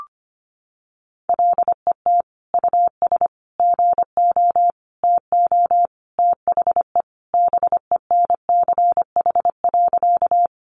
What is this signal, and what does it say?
Morse code, heard as a keyed tone, about 25 wpm, 700 Hz. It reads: LET US GO TO THE BENCH.
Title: Sentences from Top 1400 Words 25wpm